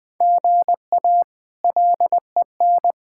Morse code, heard as a keyed tone, about 20 wpm, 700 Hz. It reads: GA LEN
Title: K1USN SST QSO Practice - 20wpm